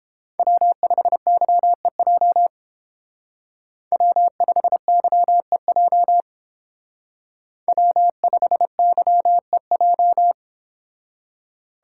Morse code, heard as a keyed tone, about 28 wpm, 700 Hz. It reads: W5YEJ W5YEJ W5YEJ